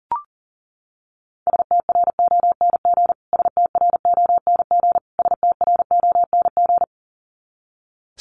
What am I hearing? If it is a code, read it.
Morse code, heard as a keyed tone, about 40 wpm, 700 Hz. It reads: STRONG STRONG STRONG
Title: Top 500 Words Repeated 40wpm